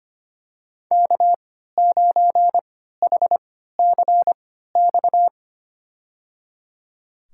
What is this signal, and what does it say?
Morse code, heard as a keyed tone, about 25 wpm, spaced as for 14 wpm, 700 Hz. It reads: K9HCX
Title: USA Call Signs - 5 Characters - 3x Character Spacing - 25wpm